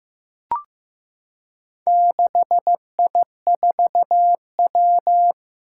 Morse code, heard as a keyed tone, about 15 wpm, 700 Hz. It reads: 6I4W
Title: Four Challenging Letter-Number Combinations 15wpm